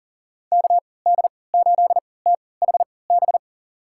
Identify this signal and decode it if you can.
Morse code, heard as a keyed tone, about 37 wpm, spaced as for 22 wpm, 700 Hz. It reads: KD8THB
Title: USA Call Signs - 6 Characters - 3x Character Spacing - 40wpm